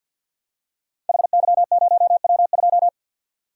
Morse code, heard as a keyed tone, about 47 wpm, 700 Hz. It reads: HY0W2
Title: Code Groups - 5 Letter:Numbers - 50wpm